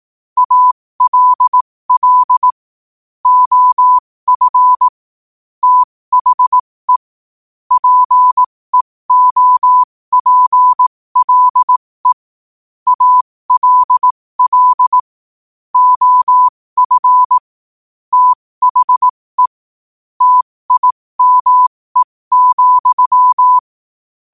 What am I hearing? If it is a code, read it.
Morse code, heard as a keyed tone, about 18 wpm, spaced as for 15 wpm, 1000 Hz. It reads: ALL OF THE PEOPLE ALL OF THE TIME,
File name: qotdfiles / 15wpm